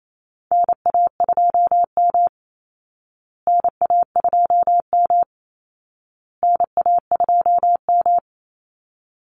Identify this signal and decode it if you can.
Morse code, heard as a keyed tone, about 28 wpm, 700 Hz. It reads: NA2M NA2M NA2M
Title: Call Signs Repeated 2x - 28wpm